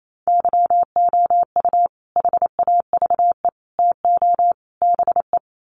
Morse code, heard as a keyed tone, about 28 wpm, 700 Hz. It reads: YOU HAVE TO BE
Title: Sets of 4 Words 28wpm